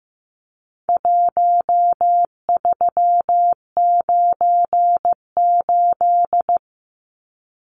Morse code, heard as a keyed tone, about 15 wpm, 700 Hz. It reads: 1398
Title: Numbers - Contest Style - 4-Digit 15wpm